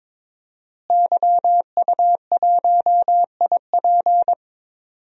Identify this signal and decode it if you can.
Morse code, heard as a keyed tone, about 21 wpm, 700 Hz. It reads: YU1IP
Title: DXCC-100 Call Signs - 5 Characters - Repeated 3x - 22wpm